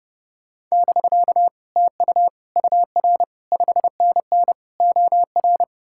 Morse code, heard as a keyed tone, about 40 wpm, 700 Hz. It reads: <BK> TU UR 5NN OR